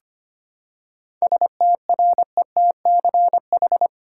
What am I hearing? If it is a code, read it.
Morse code, heard as a keyed tone, about 25 wpm, 700 Hz. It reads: STRETCH
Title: Seven Letter Words - 25wpm